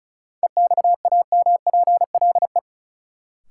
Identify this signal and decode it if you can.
Morse code, heard as a keyed tone, about 36 wpm, 700 Hz. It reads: EXAMPLE
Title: Sentences from Top 400 Words - New Word Drill - 35wpm